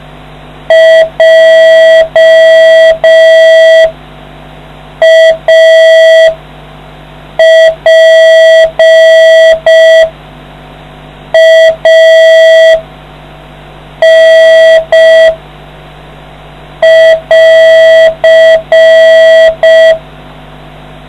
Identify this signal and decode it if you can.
Morse code, heard as a keyed tone, about 5 wpm, spaced as for 4 wpm, 650 Hz, with noise. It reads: JAPAN <AR>